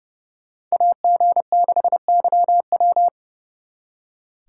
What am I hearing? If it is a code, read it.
Morse code, heard as a keyed tone, about 36 wpm, 700 Hz. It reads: AG6YW
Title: Speed Racing - USA 2x2 Call Signs 30wpm